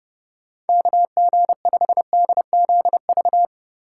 Morse code, heard as a keyed tone, about 31 wpm, 700 Hz. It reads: KG5DZV
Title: USA Call Signs - 3x Character Spacing - 30wpm